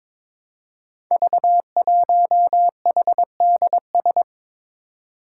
Morse code, heard as a keyed tone, about 23 wpm, 700 Hz. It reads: V1HDS
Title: Five Challenging Letter-Number Combinations 22wpm